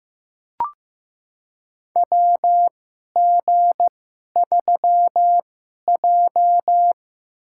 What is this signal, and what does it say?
Morse code, heard as a keyed tone, about 15 wpm, spaced as for 11 wpm, 700 Hz. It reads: WG3J
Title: USA Call Signs - 4 Characters - 2x Character Spacing - 15wpm